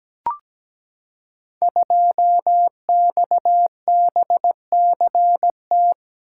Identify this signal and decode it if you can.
Morse code, heard as a keyed tone, about 17 wpm, 700 Hz. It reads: 2XBCT
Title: Code Groups - 5 Letter:Numbers - 17wpm